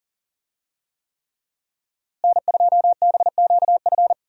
Morse code, heard as a keyed tone, about 39 wpm, 700 Hz. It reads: N2BQF
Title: USA Call Signs - 5 Characters - Repeated 3x - 40wpm